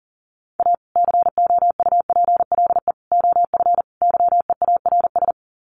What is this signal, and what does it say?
Morse code, heard as a keyed tone, about 40 wpm, 700 Hz. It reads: A COUPLE OF YEARS
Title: Sets of 4 Words 40wpm